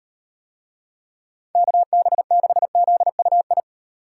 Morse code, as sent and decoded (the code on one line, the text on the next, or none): -.- -... -.... --.. ..- ..
KB6ZUI